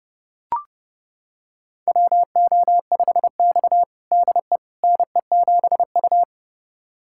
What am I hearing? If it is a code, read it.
Morse code, heard as a keyed tone, about 30 wpm, 700 Hz. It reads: WO5X DE NE7U
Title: USA Callsign DE Callsign - 4 Characters - 30wpm